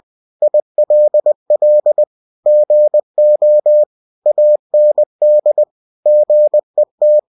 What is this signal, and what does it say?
Morse code, heard as a keyed tone, about 20 wpm, 600 Hz. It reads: ILL GO AND GET
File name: cw-books / winnie-the-pooh / winnie-20wpm-page-0049